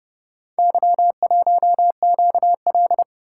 Morse code, heard as a keyed tone, about 29 wpm, 700 Hz. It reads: Y1QL